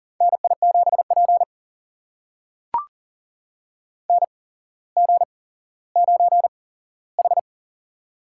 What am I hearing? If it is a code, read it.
Morse code, heard as a keyed tone, about 40 wpm, spaced as for 17 wpm, 700 Hz. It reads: NI7P NG9H
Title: USA Call Signs - 4 Characters - 8x Character Spacing - 40wpm